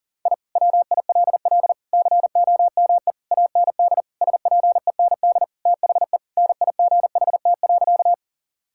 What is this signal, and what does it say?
Morse code, heard as a keyed tone, about 40 wpm, 700 Hz. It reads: I WILL COME AND SPEND THE NIGHT.
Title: Sentences from Top 1000 Words - Review Part 5 - 40wpm